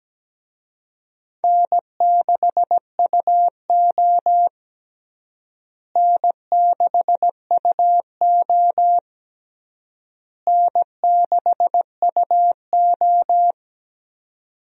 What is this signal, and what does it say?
Morse code, heard as a keyed tone, about 17 wpm, 700 Hz. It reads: N6UO N6UO N6UO
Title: USA Call Signs - 4 Characters - Repeated 3x - 17wpm